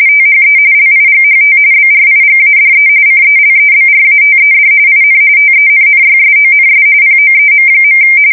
RTTY, about 45.45 baud. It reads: THE LIGHT BRIGADE CELEBRATED IN TENNYSONS POEM.
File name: fsk170 / USS-New-Jersey